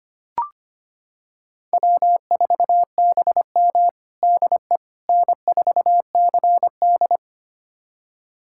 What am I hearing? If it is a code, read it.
Morse code, heard as a keyed tone, about 25 wpm, 700 Hz. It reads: W4BM DE N4CD
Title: USA Callsign DE Callsign - 4 Characters - 25wpm